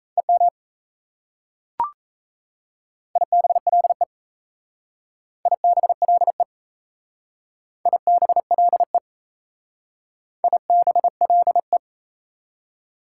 Morse code, heard as a keyed tone, about 34 wpm, 690 Hz. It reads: EM IBLE IBLE IBLE IBLE
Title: Speed-Racing Prefixes and Suffixes 28wpm